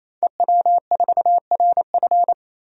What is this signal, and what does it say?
Morse code, heard as a keyed tone, about 28 wpm, 700 Hz. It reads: EW4RF
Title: DXCC-100 Call Signs - 5 Characters - 28wpm